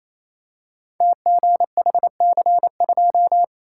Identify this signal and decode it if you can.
Morse code, heard as a keyed tone, about 28 wpm, 700 Hz. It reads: TGHC2
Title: Code Groups - 5 Letter:Numbers - 28wpm